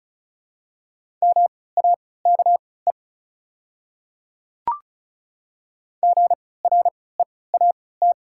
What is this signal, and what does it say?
Morse code, heard as a keyed tone, about 34 wpm, spaced as for 20 wpm, 690 Hz. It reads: MAKE GREAT
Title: 3x Character Spacing - Top 300 Words - 35wpm